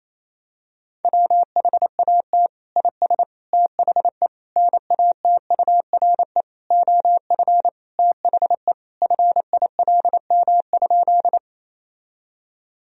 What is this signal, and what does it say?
Morse code, heard as a keyed tone, about 27 wpm, 700 Hz. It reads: WHAT IS THE NATURE OF THE FILM?
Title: Sentences from Top 1000 Words - Review Part 3 - 28wpm